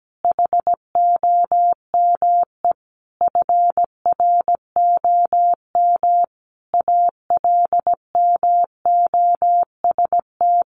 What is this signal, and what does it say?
Morse code, heard as a keyed tone, about 17 wpm, 700 Hz. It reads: HOME FROM ALMOST
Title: Sentences from Top 1400 Words 17wpm